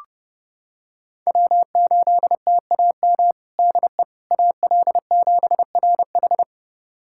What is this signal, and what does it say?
Morse code, heard as a keyed tone, about 29 wpm, 700 Hz. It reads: W8TAM DE AL7RH
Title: USA Callsign DE Callsign - 5 Characters - 30wpm